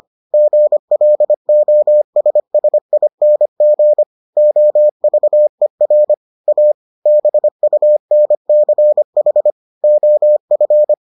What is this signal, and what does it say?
Morse code, heard as a keyed tone, about 25 wpm, 600 Hz. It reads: GLOSSING OVER A BUNCH OF